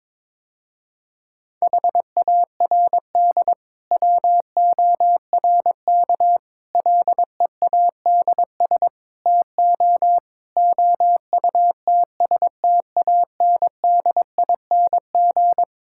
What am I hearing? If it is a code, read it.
Morse code, heard as a keyed tone, about 22 wpm, 700 Hz. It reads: HARD WORK LEADS TO OUTSTANDING
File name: Difficult Sentences with Hints - 22wpm